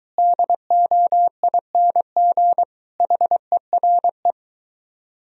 Morse code, heard as a keyed tone, about 23 wpm, 700 Hz. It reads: DOING HERE